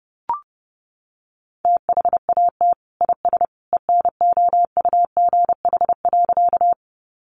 Morse code, heard as a keyed tone, about 30 wpm, 700 Hz. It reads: THAT IS ENOUGH.
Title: Sentences from Top 400 Words 30wpm